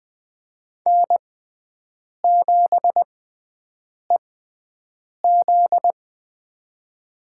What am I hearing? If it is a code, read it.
Morse code, heard as a keyed tone, about 20 wpm, spaced as for 7 wpm, 700 Hz. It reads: N7EZ
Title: USA Call Signs - 4 Characters - 6x Character Spacing - 20wpm